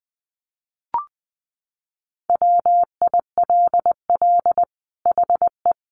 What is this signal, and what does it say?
Morse code, heard as a keyed tone, about 20 wpm, 700 Hz. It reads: WILL HE
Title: Sentences from Top 1400 Words 20wpm